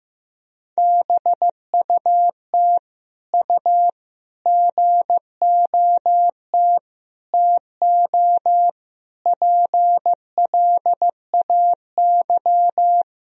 Morse code, heard as a keyed tone, about 15 wpm, 700 Hz. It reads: BUT U GOT TO PLAY
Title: Sentences from Top 300 Words and Abbreviations 15wpm